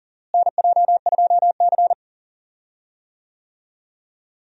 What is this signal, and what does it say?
Morse code, heard as a keyed tone, about 40 wpm, 700 Hz. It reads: NJ2C
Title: USA Call Signs - 4 Characters - Repeated 3x - 40wpm